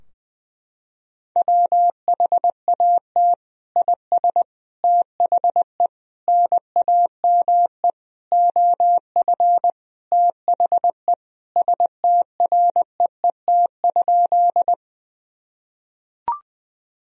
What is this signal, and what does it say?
Morse code, heard as a keyed tone, about 20 wpm, 700 Hz. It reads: WHAT IS THE NAME OF THE STREET?
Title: Sentences from Top 1000 Words - Review Part 3 - 20wpm